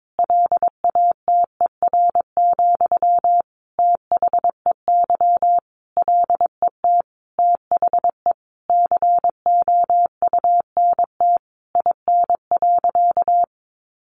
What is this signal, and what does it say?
Morse code, heard as a keyed tone, about 22 wpm, 700 Hz. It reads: LATER, THEY LET THE COUNT IN.
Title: Sentences from Top 1500 Words 22wpm